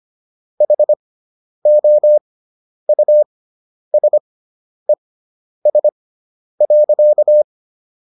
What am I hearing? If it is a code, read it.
Morse code, heard as a keyed tone, about 25 wpm, spaced as for 10 wpm, 600 Hz. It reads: HOUSES.